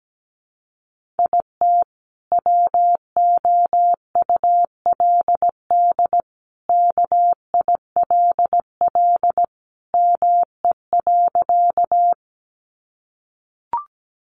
Morse code, heard as a keyed tone, about 17 wpm, 700 Hz. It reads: IT WOULD KILL ME.
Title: Sentences from Top 1400 Words 17wpm